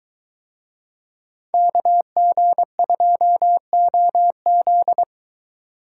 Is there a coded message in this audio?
Yes: Morse code from 1.5 to 5.0 s, heard as a keyed tone, about 22 wpm, 700 Hz. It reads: KG2OZ